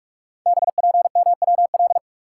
Morse code, heard as a keyed tone, about 44 wpm, 690 Hz. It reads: BPMWL
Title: Code Groups - 5 Letters - 45wpm